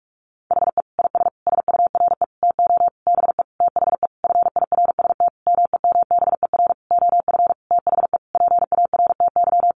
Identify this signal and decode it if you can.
Morse code, heard as a keyed tone, about 44 wpm, 700 Hz. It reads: HE IS SURE TO BE THE FIRST MEMBER OF THE PARTY
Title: Sentences from Top 1400 Words 45wpm